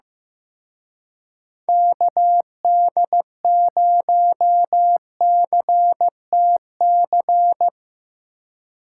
Morse code, heard as a keyed tone, about 17 wpm, 700 Hz. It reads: KD0CTC
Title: Speed Racing - USA 2x3 Call Signs 15wpm